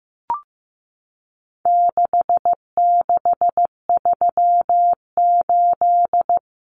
Morse code, heard as a keyed tone, about 15 wpm, 700 Hz. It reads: 6638
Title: Numbers - 4-Digit 15wpm